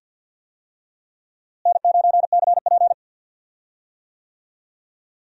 Morse code, heard as a keyed tone, about 53 wpm, 700 Hz. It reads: N9CP
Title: USA Call Signs - 4 Characters - Repeated 2x - 50wpm